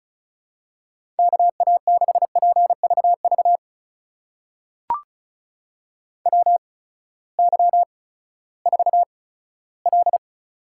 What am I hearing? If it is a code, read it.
Morse code, heard as a keyed tone, about 36 wpm, spaced as for 18 wpm, 700 Hz. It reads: KA6PVV WY4L
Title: USA Call Signs - 8x Character Spacing - 35wpm